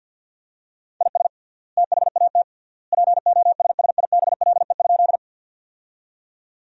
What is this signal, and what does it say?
Morse code, heard as a keyed tone, about 49 wpm, 700 Hz. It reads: IS THAT POSSIBLE?
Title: 3x Word-Spacing - Sentences from Top 500 Words - 50wpm